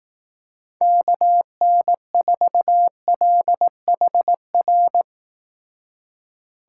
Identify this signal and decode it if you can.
Morse code, heard as a keyed tone, about 19 wpm, 700 Hz. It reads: KN4LHR
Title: Speed Racing - USA Call Signs 15wpm